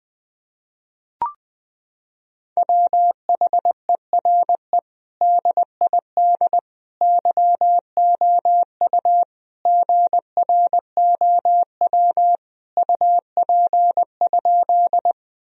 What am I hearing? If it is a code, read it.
Morse code, heard as a keyed tone, about 20 wpm, 700 Hz. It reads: WHERE DID YOU GROW UP?
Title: Sentences from Top 1000 Words - Review Part 1 - 20wpm